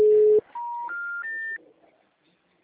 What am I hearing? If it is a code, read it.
Special information tone.
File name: out- -2006- - - .140